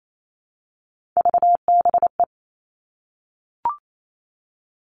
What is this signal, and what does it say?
Morse code, heard as a keyed tone, about 28 wpm, 700 Hz. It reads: VBE
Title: Three Challenging Letter-Number Combinations 28wpm